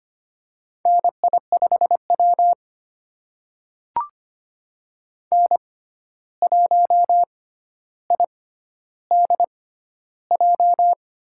Morse code, heard as a keyed tone, about 25 wpm, spaced as for 12 wpm, 700 Hz. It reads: NI5W N1IDJ
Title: USA Call Signs - 6x Character Spacing - 25wpm